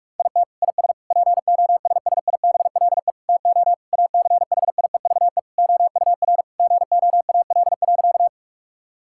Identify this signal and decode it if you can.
Morse code, heard as a keyed tone, about 46 wpm, 700 Hz. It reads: IT IS POSSIBLE TO ACHIEVE OUR GOAL.
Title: Sentences from Top 1000 Words - Review Part 6 - 45wpm